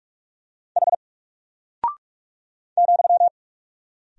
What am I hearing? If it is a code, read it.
Morse code, heard as a keyed tone, about 44 wpm, 700 Hz. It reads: H ,
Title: Single Character (Letter:Number:Punctuation:Prosign) Alt 45wpm